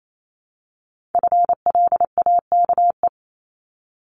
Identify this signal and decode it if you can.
Morse code, heard as a keyed tone, about 28 wpm, 700 Hz. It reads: FLAKE
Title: Multi-letter Phonemes 28wpm